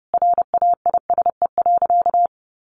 Morse code, heard as a keyed tone, about 30 wpm, 700 Hz. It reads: RAISE.
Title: Easy Phrases 30wpm